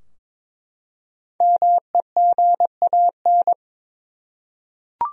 Morse code, heard as a keyed tone, about 22 wpm, 700 Hz. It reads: MEGAN
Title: USA Common Names - Max 5 Letters - 22wpm